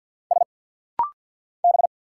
Morse code, heard as a keyed tone, about 50 wpm, 700 Hz. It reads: S B
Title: Single Letter-Number Alt - Rapid-Fire 50wpm